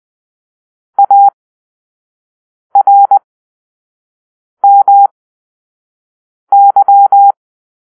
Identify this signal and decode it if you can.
Morse code, heard as a keyed tone, about 20 wpm, spaced as for 5 wpm, 800 Hz. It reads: ARMY
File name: Grp-3 Drill-1